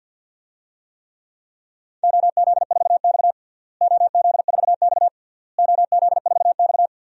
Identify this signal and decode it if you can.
Morse code, heard as a keyed tone, about 48 wpm, 700 Hz. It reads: OZ4X OZ4X OZ4X
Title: DXCC-100 Call Signs - 4 Characters - Repeated 3x - 50wpm